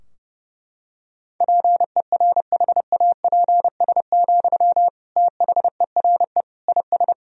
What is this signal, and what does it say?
Morse code, heard as a keyed tone, about 30 wpm, 700 Hz. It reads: PERHAPS, THERE IS
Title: Sentences from Top 1000 Words - Review Part 4 - 30wpm